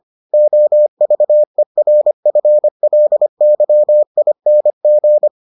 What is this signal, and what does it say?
Morse code, heard as a keyed tone, about 25 wpm, 600 Hz. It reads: OVERFLYING